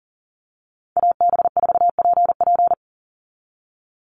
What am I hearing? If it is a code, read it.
Morse code, heard as a keyed tone, about 40 wpm, 700 Hz. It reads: AB4PP
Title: NA SOTA Chaser Callsigns 40wpm